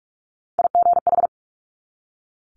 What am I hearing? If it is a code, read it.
Morse code, heard as a keyed tone, about 46 wpm, 700 Hz. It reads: IGH
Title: Multi-letter Phonemes 45wpm